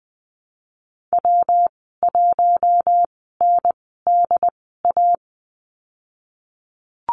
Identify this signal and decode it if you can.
Morse code, heard as a keyed tone, about 20 wpm, spaced as for 15 wpm, 700 Hz. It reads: W1NDA
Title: 147 Lesson 24g - Letter y - Call Signs 20wpm